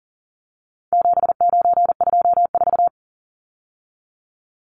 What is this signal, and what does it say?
Morse code, heard as a keyed tone, about 40 wpm, 700 Hz. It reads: 7924
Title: Numbers - 4-Digit 40wpm